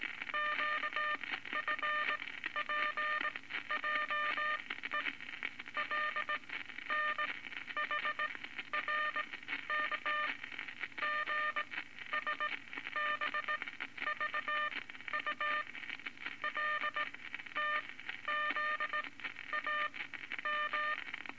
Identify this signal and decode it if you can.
Morse code, heard as a keyed tone, about 19 wpm, spaced as for 12 wpm, 1200 Hz, with noise. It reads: QFPJE LNHRK GSBVU LTZAM
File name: Morse Code Records / 19 A LCM 166 Ten Words Per Minute